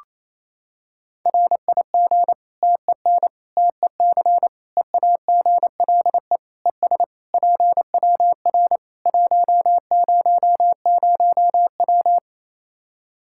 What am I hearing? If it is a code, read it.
Morse code, heard as a keyed tone, about 28 wpm, 700 Hz. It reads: RIG TEN TEC EAGLE ES PWR 100W
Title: Practice QSO - Exchange 2 - Info - 28wpm